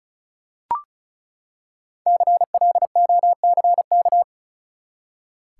Morse code, heard as a keyed tone, about 36 wpm, 700 Hz. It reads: CLOCK
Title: Five Letter Words - 35wpm